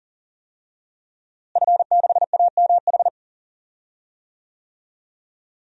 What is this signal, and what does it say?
Morse code, heard as a keyed tone, about 40 wpm, 700 Hz. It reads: F6AMH
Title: DXCC-100 Call Signs - 5 Characters - Repeated 2x - 40wpm